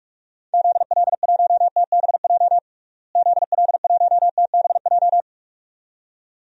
Morse code, heard as a keyed tone, about 46 wpm, 700 Hz. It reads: ZL1TBJ ZL1TBJ
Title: DXCC-100 Call Signs - 6 Characters - Repeated 2x - 45wpm